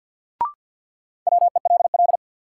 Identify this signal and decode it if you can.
Morse code, heard as a keyed tone, about 53 wpm, 700 Hz. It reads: WELL
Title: Sentences from Top 200 Words - New Word Drill - 50wpm